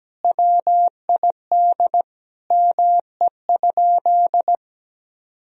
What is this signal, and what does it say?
Morse code, heard as a keyed tone, about 17 wpm, 700 Hz. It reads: WID ME?
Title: Sentences from Top 400 Words and Abbreviations 17wpm